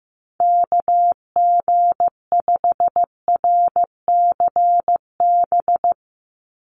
Morse code, heard as a keyed tone, about 15 wpm, 700 Hz. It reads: KG5RCB
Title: USA Call Signs 15wpm